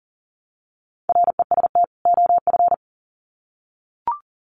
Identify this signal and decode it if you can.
Morse code, heard as a keyed tone, about 40 wpm, 700 Hz. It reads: REST OF
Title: Sets of 2 Words 40wpm